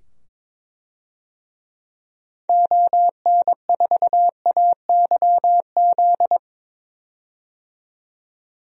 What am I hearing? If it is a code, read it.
Morse code, heard as a keyed tone, about 22 wpm, 700 Hz. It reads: ON4AYZ
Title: DXCC-100 Call Signs - 6 Characters - 22wpm